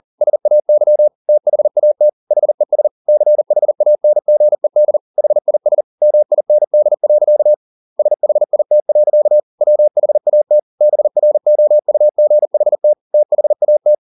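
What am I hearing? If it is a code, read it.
Morse code, heard as a keyed tone, about 40 wpm, 600 Hz. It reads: SAY THAT HES CHANGED HIS MIND. SHIT. WHAT BROUGHT THAT
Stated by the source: Morse code at 40 wpm, practice text, CW med Gnister.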